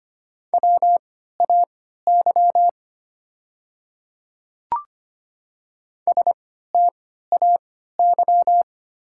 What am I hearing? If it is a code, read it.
Morse code, heard as a keyed tone, about 25 wpm, spaced as for 14 wpm, 700 Hz. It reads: WAY STAY
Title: 3x Character Spacing - Top 500 Words - 25wpm